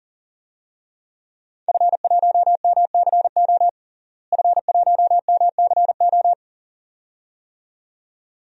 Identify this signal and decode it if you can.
Morse code, heard as a keyed tone, about 40 wpm, 700 Hz. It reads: F1MCO F1MCO
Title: DXCC-100 Call Signs - 5 Characters - Repeated 2x - 40wpm